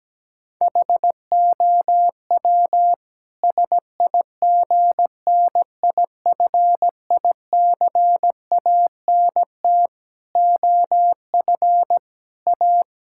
Morse code, heard as a keyed tone, about 17 wpm, 700 Hz. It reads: HOW SIGNIFICANT OF A
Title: Sentences from Top 1000 Words - Review Part 6 - 17wpm